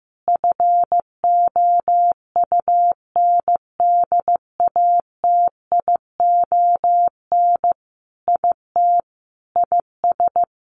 Morse code, heard as a keyed tone, about 15 wpm, 700 Hz. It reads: FOUNDATION IT IS
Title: Sentences from Top 1200 Words 15wpm